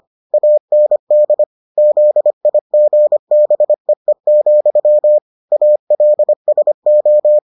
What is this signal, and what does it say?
Morse code, heard as a keyed tone, about 25 wpm, 600 Hz. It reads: AND ZIGBEE, ALSO